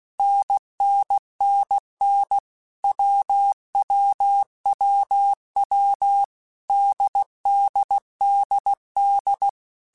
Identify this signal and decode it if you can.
Morse code, heard as a keyed tone, about 16 wpm, 800 Hz. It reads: NNNN WWWW DDDD
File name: Basic Letter Groups By7Groups / AWUNDG / AWUNDG@15